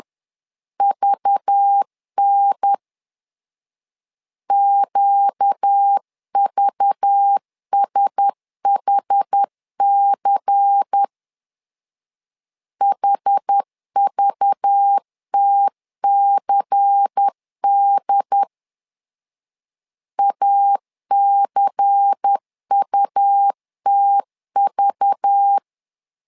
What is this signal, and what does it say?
Morse code, heard as a keyed tone, about 10 wpm, 800 Hz. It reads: VN QVSHC HVTCD ACUTV
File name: LEZIONI / 5 LBVC / 5 LBVC 10wpm 2349